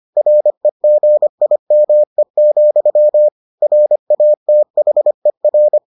Morse code, heard as a keyed tone, about 25 wpm, 600 Hz. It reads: REGIME, RATHER